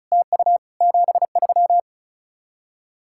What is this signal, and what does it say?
Morse code, heard as a keyed tone, about 44 wpm, 700 Hz. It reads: TU 73
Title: Speed-Racing Common QSO Phrases - 35wpm